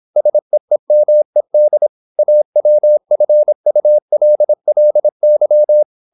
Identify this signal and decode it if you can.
Morse code, heard as a keyed tone, about 26 wpm, 600 Hz. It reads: SEEMED AWFULLY
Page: Never 26wpm | CW med Gnister